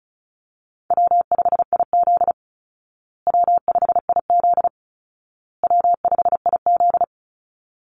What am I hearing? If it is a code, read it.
Morse code, heard as a keyed tone, about 34 wpm, 700 Hz. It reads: W5IZ W5IZ W5IZ
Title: Easy Call Signs Repeated 3x - 35wpm